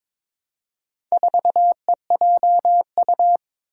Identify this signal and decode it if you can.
Morse code, heard as a keyed tone, about 22 wpm, 700 Hz. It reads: 4EJU
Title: Four Challenging Letter-Number Combinations 22wpm